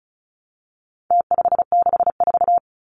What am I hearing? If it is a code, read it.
Morse code, heard as a keyed tone, about 34 wpm, 700 Hz. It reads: T564
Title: Numbers - Contest Style - 4-Digit 35wpm